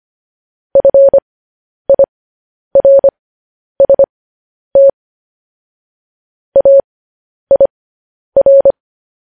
Morse code, heard as a keyed tone, about 25 wpm, spaced as for 10 wpm, 550 Hz. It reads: FIRST AIR